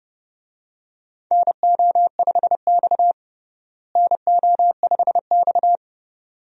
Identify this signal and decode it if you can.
Morse code, heard as a keyed tone, about 29 wpm, 700 Hz. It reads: NO5X NO5X
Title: USA Call Signs - 4 Characters - Repeated 2x - 30wpm